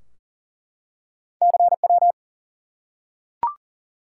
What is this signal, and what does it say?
Morse code, heard as a keyed tone, about 42 wpm, 700 Hz. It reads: CW
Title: Top 180 QSO Elements 40wpm